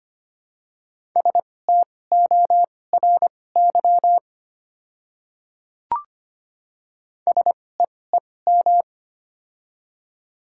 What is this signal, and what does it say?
Morse code, heard as a keyed tone, about 25 wpm, spaced as for 18 wpm, 700 Hz. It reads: STORY SEEM
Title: 2x Character Spacing - Top 500 Words - 25wpm